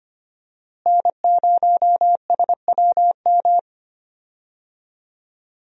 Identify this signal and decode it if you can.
Morse code, heard as a keyed tone, about 25 wpm, 690 Hz. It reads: N0SWM
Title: USA Call Signs - 5 Characters - Repeated 2x - 25wpm